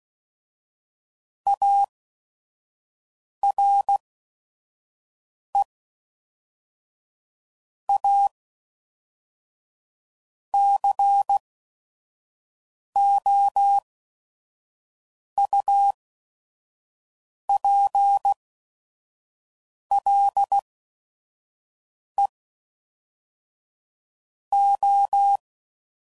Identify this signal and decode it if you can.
Morse code, heard as a keyed tone, about 16 wpm, spaced as for 5 wpm, 800 Hz. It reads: ARE A COUPLE O
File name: CW Code Groups @5WPM